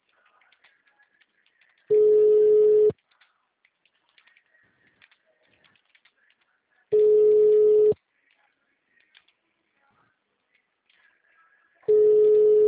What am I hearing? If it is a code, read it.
Ringback tone (Europe, 425 Hz).